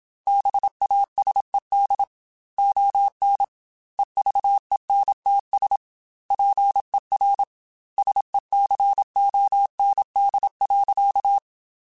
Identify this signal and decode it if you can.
Morse code, heard as a keyed tone, about 27 wpm, 800 Hz. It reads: BASED ON EVENTS PER SECOND.
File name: Technical Subjects / Technical Subjects-mp3s / Reactance @25